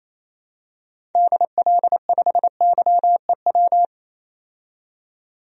Morse code, heard as a keyed tone, about 28 wpm, 700 Hz. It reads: DL5YEW
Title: DXCC-100 Call Signs - 6 Characters - 28wpm